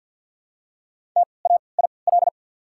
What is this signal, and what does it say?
Morse code, heard as a keyed tone, about 53 wpm, spaced as for 29 wpm, 700 Hz. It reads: TAIL
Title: 3x Character Spacing - Top 500 Words - 50wpm